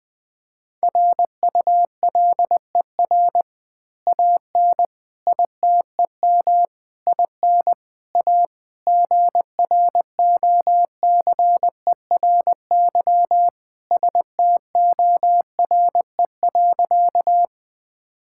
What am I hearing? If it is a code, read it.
Morse code, heard as a keyed tone, about 20 wpm, 700 Hz. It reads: RULER AN ITEM IN A GROCERY STORE.
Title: Easy Words with Hints - 20wpm